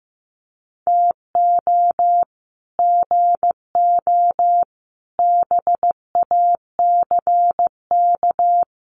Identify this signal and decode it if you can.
Morse code, heard as a keyed tone, about 15 wpm, 700 Hz. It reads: TO GO BACK
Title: Sets of 4 Words 15wpm